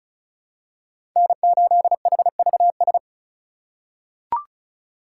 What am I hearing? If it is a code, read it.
Morse code, heard as a keyed tone, about 36 wpm, 700 Hz. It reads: N8HVS
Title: USA Call Signs - 5 Characters - 4x Character Spacing - 35wpm